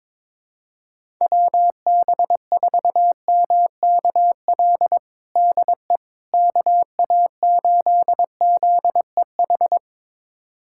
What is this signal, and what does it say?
Morse code, heard as a keyed tone, about 23 wpm, 700 Hz. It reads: WB4MKL DE KA8ZEH
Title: USA Callsign DE Callsign - 6 Characters - 22wpm